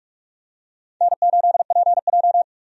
Morse code, heard as a keyed tone, about 46 wpm, 700 Hz. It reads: N8PJ
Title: USA Call Signs - 4 Characters - 45wpm